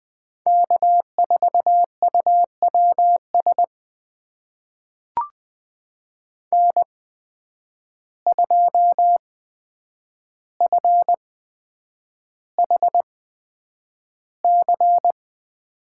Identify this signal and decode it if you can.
Morse code, heard as a keyed tone, about 21 wpm, spaced as for 9 wpm, 700 Hz. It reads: K4UWS N2FHC
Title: USA Call Signs - 5 Characters - 8x Character Spacing - 20wpm